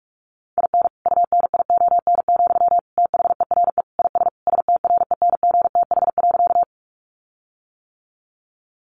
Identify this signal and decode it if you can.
Morse code, heard as a keyed tone, about 44 wpm, 700 Hz. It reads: IN UNION, THERE IS STRENGTH.
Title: Sentences from Top 1400 Words 45wpm